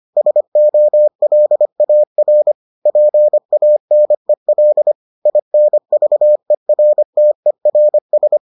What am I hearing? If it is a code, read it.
Morse code, heard as a keyed tone, about 25 wpm, 600 Hz. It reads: SOLAR PANEL INVERTERS